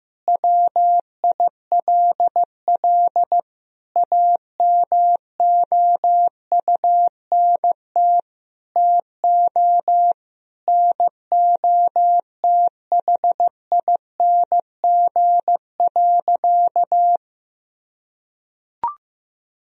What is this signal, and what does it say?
Morse code, heard as a keyed tone, about 15 wpm, 700 Hz. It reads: WILL AMOUNT TO NOTHING.
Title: Sentences from Top 1000 Words - Review Part 4 - 15wpm